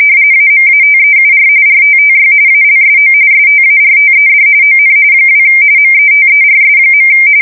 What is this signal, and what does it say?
RTTY, about 45.45 baud. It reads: AMATEUR RTTY ALSO STARTED ON 2 METERS.
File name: fsk170 / teletype-RTTY